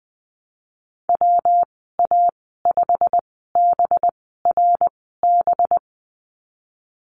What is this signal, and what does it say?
Morse code, heard as a keyed tone, about 20 wpm, spaced as for 15 wpm, 700 Hz. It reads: WA5BRB
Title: 219 Lesson 35g - Number 0 - Call Signs 20wpm